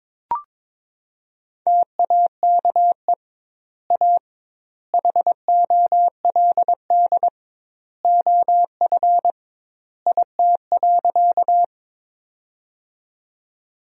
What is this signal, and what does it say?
Morse code, heard as a keyed tone, about 22 wpm, 700 Hz. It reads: TAKE A HOLD OF IT.
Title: 2x Word-Spacing - Sentences from Top 400 Words - 22wpm